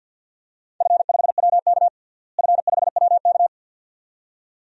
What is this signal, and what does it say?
Morse code, heard as a keyed tone, about 52 wpm, 700 Hz. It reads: F5WK F5WK
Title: DXCC-100 Call Signs - 4 Characters - Repeated 2x - 50wpm